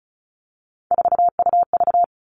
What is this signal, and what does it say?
Morse code, heard as a keyed tone, about 34 wpm, 700 Hz. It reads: 4UV
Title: All Cut Numbers - 3-Digit 35wpm